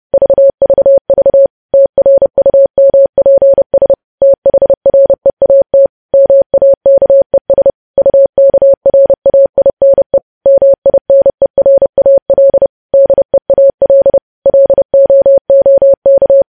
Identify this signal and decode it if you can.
Morse code, heard as a keyed tone, about 30 wpm, 550 Hz. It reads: VVV TRUMPS THREAT MAKES UKRAINE MINERAL DEAL LOOK
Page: news headlines in morse code at 30 wpm